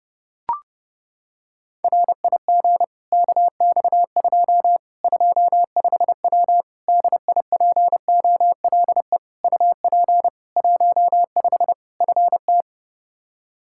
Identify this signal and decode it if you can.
Morse code, heard as a keyed tone, about 30 wpm, 700 Hz. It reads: RIG KX2 25W DIPOLE UP 15 FT
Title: Practice QSO - Exchange 2 - Info - 30wpm